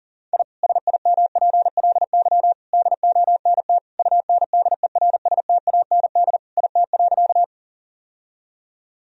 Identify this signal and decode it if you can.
Morse code, heard as a keyed tone, about 37 wpm, 700 Hz. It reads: I SIMPLY DONT UNDERSTAND IT.
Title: Sentences from Top 1000 Words - Review Part 4 - 40wpm